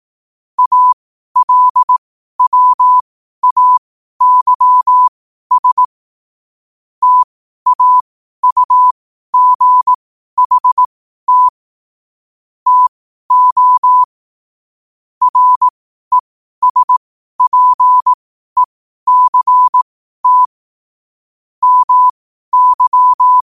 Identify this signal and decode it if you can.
Morse code, heard as a keyed tone, about 18 wpm, spaced as for 12 wpm, 1000 Hz. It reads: ALWAYS TAUGHT TO RESPECT MY
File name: qotdfiles / 12wpm